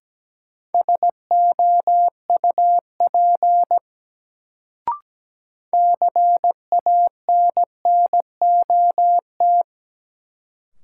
Morse code, heard as a keyed tone, about 17 wpm, 700 Hz. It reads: SOUP CANNOT
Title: Sentences from Top 1000 Words - Review Part 6 - New Word Drill - 17wpm